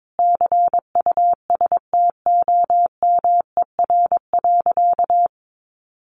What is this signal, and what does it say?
Morse code, heard as a keyed tone, about 22 wpm, 700 Hz. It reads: CUSTOMER.
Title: Sentences from Top 1100 Words 22wpm